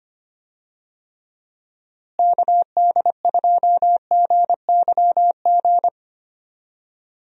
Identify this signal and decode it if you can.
Morse code, heard as a keyed tone, about 25 wpm, 700 Hz. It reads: KD2GYG
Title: USA Call Signs - 6 Characters - Repeated 2x - 25wpm